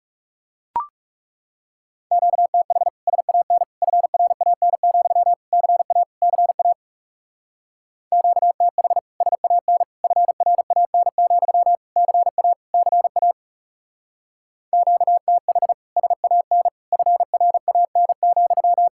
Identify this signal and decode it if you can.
Morse code, heard as a keyed tone, about 37 wpm, 700 Hz. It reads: QTH SAN FRAN, CA CA QTH SAN FRAN, CA CA QTH SAN FRAN,
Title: Speed-Racing Common QSO Phrases Encore - 30wpm